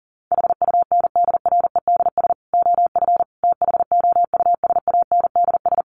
Morse code, heard as a keyed tone, about 40 wpm, 700 Hz. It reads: HUNDREDS OF THOUSANDS
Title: Sets of 4 Words 40wpm